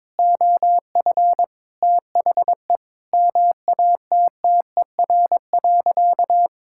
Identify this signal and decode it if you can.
Morse code, heard as a keyed tone, about 22 wpm, 700 Hz. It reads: OF THE MATTER.
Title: Sentences from Top 1600 Words 22wpm